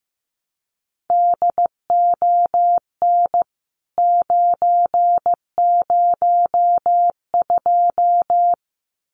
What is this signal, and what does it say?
Morse code, heard as a keyed tone, about 15 wpm, 700 Hz. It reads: DON 902
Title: CWT Exchange 15wpm